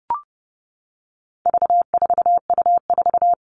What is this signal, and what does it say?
Morse code, heard as a keyed tone, about 30 wpm, 700 Hz. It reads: V4U4
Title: All Cut Numbers - 4-Digit 30wpm